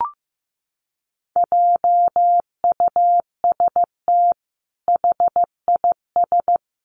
Morse code, heard as a keyed tone, about 15 wpm, 700 Hz. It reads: JUST HIS
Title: Sentences from Top 1500 Words 15wpm